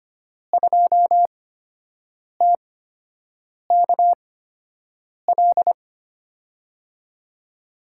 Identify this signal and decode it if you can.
Morse code, heard as a keyed tone, about 25 wpm, spaced as for 7 wpm, 700 Hz. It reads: 2TKL
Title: USA Call Signs - 5 Characters - 8x Character Spacing - 25wpm